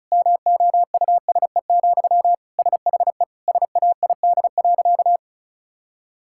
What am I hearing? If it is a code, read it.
Morse code, heard as a keyed tone, about 34 wpm, 700 Hz. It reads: MOUSE, SHE SAID.
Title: Sentences from Top 1000 Words - Review Part 5 - 35wpm